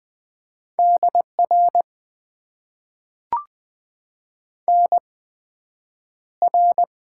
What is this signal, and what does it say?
Morse code, heard as a keyed tone, about 20 wpm, spaced as for 9 wpm, 700 Hz. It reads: DR NR
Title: Top 90 QSO Elements - 8x Character Spacing - 20wpm